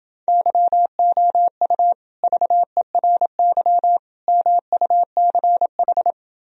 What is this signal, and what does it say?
Morse code, heard as a keyed tone, about 29 wpm, 700 Hz. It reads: YOU VERY MUCH